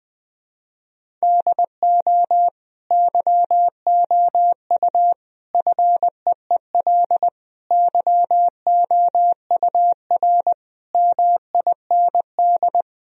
Morse code, heard as a keyed tone, about 20 wpm, 700 Hz. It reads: DO YOU FEEL YOUR MIND
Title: Sentences from Top 1000 Words - Review Part 1 - 20wpm